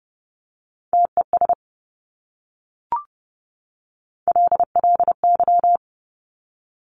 Morse code, heard as a keyed tone, about 30 wpm, 690 Hz. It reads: TES LLY
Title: Top Three Letter Combinations in Words 30wpm